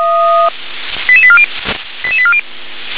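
Five-tone selective call (ZVEI), heard as quick runs of five tones, tones of about 70 ms. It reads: 88840 88840
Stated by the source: ZVEI five-tone selective call received on an SDR.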